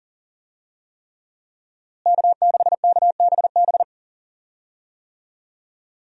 Morse code, heard as a keyed tone, about 40 wpm, 700 Hz. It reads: K6KBB
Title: USA Call Signs - 5 Characters - Repeated 3x - 40wpm